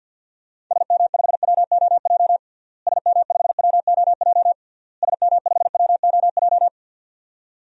Morse code, heard as a keyed tone, about 52 wpm, 700 Hz. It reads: SM5WOJ SM5WOJ SM5WOJ
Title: DXCC-100 Call Signs - 6 Characters - Repeated 3x - 50wpm